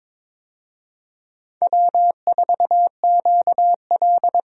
Morse code, heard as a keyed tone, about 22 wpm, 700 Hz. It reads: W4QL
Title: USA Call Signs - 4 Characters - 22wpm